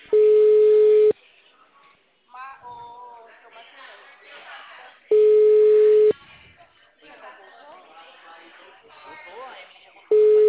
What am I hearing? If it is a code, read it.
Ringback tone (Europe, 425 Hz), with speech.